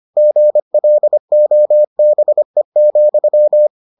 Morse code, heard as a keyed tone, about 25 wpm, 600 Hz. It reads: GLOBE,